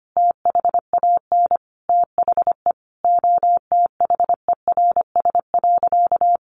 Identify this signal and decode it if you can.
Morse code, heard as a keyed tone, about 25 wpm, 700 Hz. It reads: THAN THE OTHERS.
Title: Sentences from Top 1500 Words 25wpm